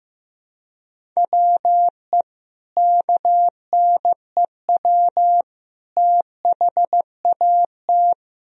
Morse code, heard as a keyed tone, about 15 wpm, 700 Hz. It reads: WE KNEW THAT
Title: Sentences from Top 400 Words and Abbreviations 15wpm